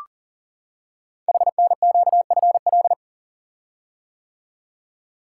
Morse code, heard as a keyed tone, about 38 wpm, 700 Hz. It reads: HNQFL
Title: Code Groups - 5 Letters - 40wpm